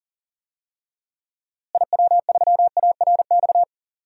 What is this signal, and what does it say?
Morse code, heard as a keyed tone, about 40 wpm, 700 Hz. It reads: IW3ARX
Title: DXCC-100 Call Signs - 6 Characters - Repeated 3x - 40wpm